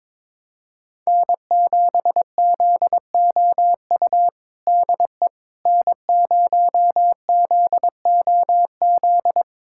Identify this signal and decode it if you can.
Morse code, heard as a keyed tone, about 22 wpm, 700 Hz. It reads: N7ZOU DE N0ZOZ
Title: USA Callsign DE Callsign - 5 Characters - 22wpm